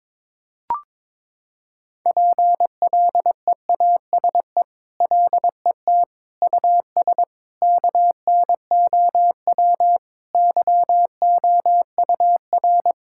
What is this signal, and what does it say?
Morse code, heard as a keyed tone, about 21 wpm, 700 Hz. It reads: PLEASE LET US KNOW YOUR
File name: practice / 22wpm / Sentences with 8-Word Phrases - 22wpm